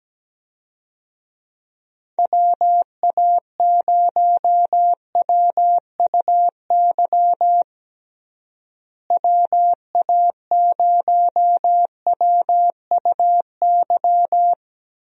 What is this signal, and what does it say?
Morse code, heard as a keyed tone, about 17 wpm, 700 Hz. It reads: WA0WUY WA0WUY
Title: USA Call Signs - 6 Characters - Repeated 3x - 17wpm